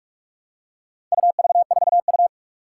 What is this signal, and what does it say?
Morse code, heard as a keyed tone, about 46 wpm, 700 Hz. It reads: UV4U
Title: Four Challenging Letter-Number Combinations 45wpm